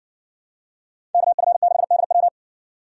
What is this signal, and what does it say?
Morse code, heard as a keyed tone, about 60 wpm, 700 Hz. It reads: KF6DW